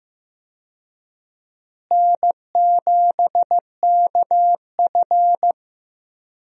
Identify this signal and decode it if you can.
Morse code, heard as a keyed tone, about 15 wpm, 700 Hz. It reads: N7KF
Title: USA Call Signs - 4 Characters - Repeated 3x - 15wpm